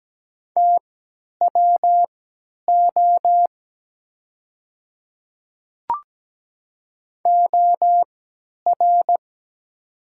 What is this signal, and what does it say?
Morse code, heard as a keyed tone, about 17 wpm, spaced as for 10 wpm, 700 Hz. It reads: TWO OR
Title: 3x Character Spacing - Top 100 Words - 17wpm